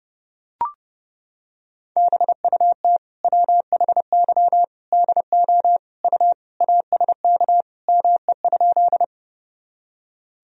Morse code, heard as a keyed tone, about 30 wpm, 700 Hz. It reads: BUT WHY DO U ASK ME?
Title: Sentences from Top 200 Words and Abbreviations 30wpm